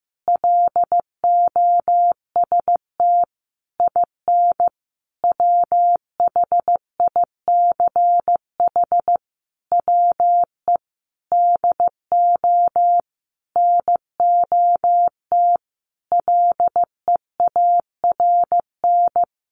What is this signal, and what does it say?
Morse code, heard as a keyed tone, about 14 wpm, 700 Hz. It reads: LOST IN WHICH WE DO NOT LEARN
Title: Sentences from Top 1400 Words 15wpm